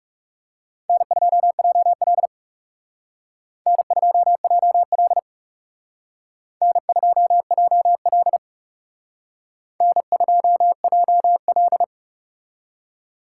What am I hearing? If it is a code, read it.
Morse code, heard as a keyed tone, about 36 wpm, 700 Hz. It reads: N2JL N2JL N2JL N2JL
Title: Speed Racing - USA 1x2 Call Signs 30wpm